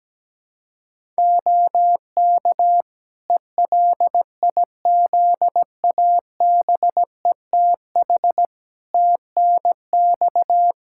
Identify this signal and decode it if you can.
Morse code, heard as a keyed tone, about 17 wpm, 700 Hz. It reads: OK ELIZABETH TNX
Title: Practice QSO - Exchange 4 - Ending - 17wpm